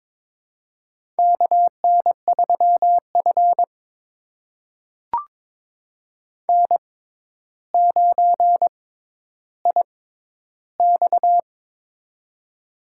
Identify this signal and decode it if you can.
Morse code, heard as a keyed tone, about 22 wpm, spaced as for 11 wpm, 700 Hz. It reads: KN3F N9IX
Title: USA Call Signs - 4 Characters - 6x Character Spacing - 22wpm